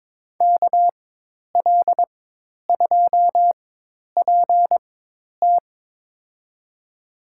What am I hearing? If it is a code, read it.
Morse code, heard as a keyed tone, about 22 wpm, spaced as for 10 wpm, 700 Hz. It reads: KL2PT
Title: USA Call Signs - 4x Character Spacing - 22wpm